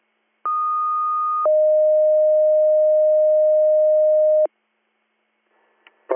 Two-tone page, heard as a one-second tone, then a three-second tone, about 1230 then 620 Hz. It is heard from 0.4 to 4.5 s.